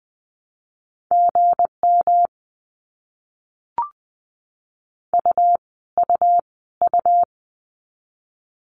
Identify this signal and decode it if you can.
Morse code, heard as a keyed tone, about 20 wpm, 700 Hz. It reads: GM U U U
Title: Abbreviations Repeated 3x - 20wpm